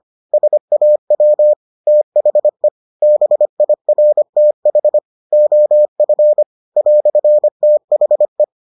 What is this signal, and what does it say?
Morse code, heard as a keyed tone, about 25 wpm, 600 Hz. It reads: SAW THE BIRTH OF "THE